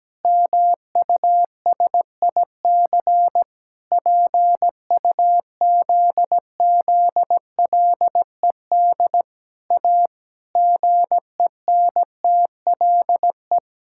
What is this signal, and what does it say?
Morse code, heard as a keyed tone, about 17 wpm, 700 Hz. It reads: MUSIC PUZZLED A GENTLE
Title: Pangram Challenge v2 - 17wpm